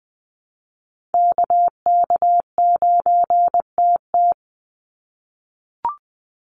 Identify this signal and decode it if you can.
Morse code, heard as a keyed tone, about 20 wpm, 700 Hz. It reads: KK9TT
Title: USA Call Signs 20wpm